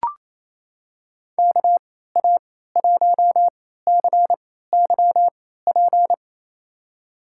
Morse code, heard as a keyed tone, about 28 wpm, spaced as for 16 wpm, 700 Hz. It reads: KA1CYP
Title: USA Call Signs - 3x Character Spacing - 28wpm